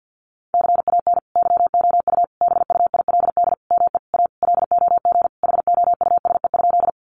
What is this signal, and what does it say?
Morse code, heard as a keyed tone, about 50 wpm, 700 Hz. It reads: CAN YOU BUILD ME A LOG HOUSE?
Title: Sentences from Top 1400 Words 50wpm